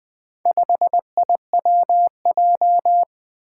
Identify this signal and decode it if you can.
Morse code, heard as a keyed tone, about 20 wpm, 700 Hz. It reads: 5IWJ
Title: Four Challenging Letter-Number Combinations 20wpm